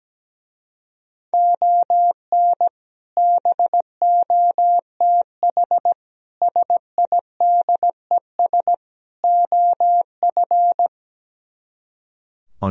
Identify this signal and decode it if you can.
Morse code, heard as a keyed tone, about 17 wpm, 700 Hz. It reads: ON BOTH SIDES OF
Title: Speed-Racing - Sets of 4 Words 17wpm